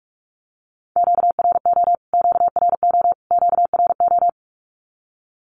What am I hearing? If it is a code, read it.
Morse code, heard as a keyed tone, about 44 wpm, 700 Hz. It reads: QRO QRO QRO
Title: Q-Codes Repeated 3x 45wpm